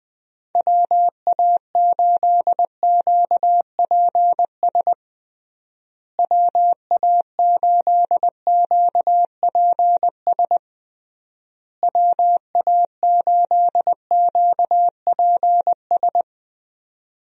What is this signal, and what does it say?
Morse code, heard as a keyed tone, about 21 wpm, 700 Hz. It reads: WA8QPS WA8QPS WA8QPS
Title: USA Call Signs - 6 Characters - Repeated 3x - 20wpm